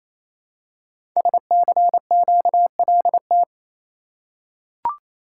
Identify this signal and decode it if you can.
Morse code, heard as a keyed tone, about 27 wpm, 700 Hz. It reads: SCQLT
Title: Code Groups - 5 Letters - 28wpm